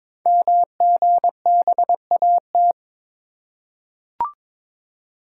Morse code, heard as a keyed tone, about 22 wpm, 700 Hz. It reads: MGBAT
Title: Code Groups - 5 Letter:Numbers - 22wpm